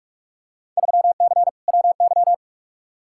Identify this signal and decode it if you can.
Morse code, heard as a keyed tone, about 47 wpm, 700 Hz. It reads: 3C WY
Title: Field Day Exchanges Encore - 45wpm